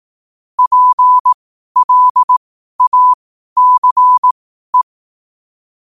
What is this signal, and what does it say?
Morse code, heard as a keyed tone, about 18 wpm, spaced as for 12 wpm, 1000 Hz. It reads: PLACE
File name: qotdfiles / 12wpm